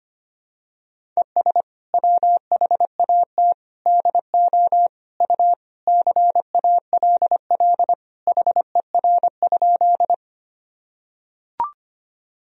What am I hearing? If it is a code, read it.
Morse code, heard as a keyed tone, about 25 wpm, 700 Hz. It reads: ES WHAT DO U CALL HER?
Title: Sentences from Top 100 Words and Abbreviations 25wpm